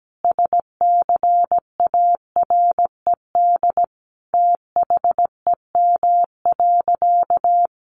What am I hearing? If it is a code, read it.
Morse code, heard as a keyed tone, about 17 wpm, 700 Hz. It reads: SCARED THEM.
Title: Sentences from Top 700 Words 17wpm